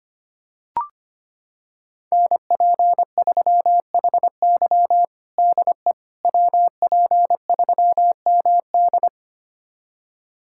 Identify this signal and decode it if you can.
Morse code, heard as a keyed tone, about 25 wpm, 700 Hz. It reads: NP3HY DE WP3MD
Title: USA Callsign DE Callsign - 5 Characters - 25wpm